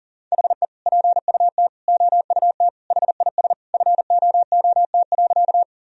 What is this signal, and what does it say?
Morse code, heard as a keyed tone, about 40 wpm, 700 Hz. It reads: HE PUT OUT HIS FOOT.
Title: Sentences from Top 1000 Words - Review Part 3 - 40wpm